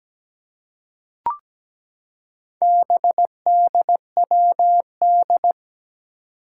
Morse code, heard as a keyed tone, about 17 wpm, 700 Hz. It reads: BDWD
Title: Four Challenging Letter-Number Combinations 17wpm